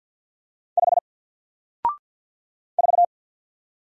Morse code, heard as a keyed tone, about 50 wpm, 700 Hz. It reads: 5 4
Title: Single Letter-Number 50wpm